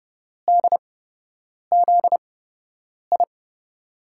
Morse code, heard as a keyed tone, about 31 wpm, spaced as for 8 wpm, 700 Hz. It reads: DZI
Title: USA Call Signs - 6 Characters - 8x Character Spacing - 30wpm